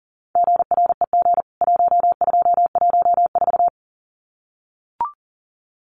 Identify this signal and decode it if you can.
Morse code, heard as a keyed tone, about 40 wpm, 700 Hz. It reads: GREG 1214
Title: CWT Exchange 40wpm